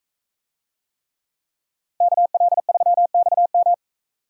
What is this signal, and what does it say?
Morse code, heard as a keyed tone, about 42 wpm, 700 Hz. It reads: KL3XM